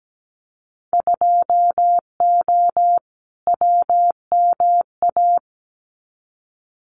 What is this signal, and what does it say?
Morse code, heard as a keyed tone, about 17 wpm, 700 Hz. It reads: 2O WMA
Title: Winter Field Day Exchanges Encore 17wpm